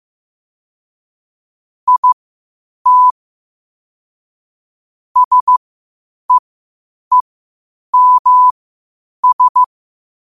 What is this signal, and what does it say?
Morse code, heard as a keyed tone, about 15 wpm, spaced as for 8 wpm, 1000 Hz. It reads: IT SEEMS
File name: qotdfiles / 8wpm